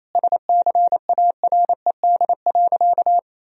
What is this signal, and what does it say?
Morse code, heard as a keyed tone, about 28 wpm, 700 Hz. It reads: SCARED.
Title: Sentences from Top 1000 Words - Review Part 4 - 28wpm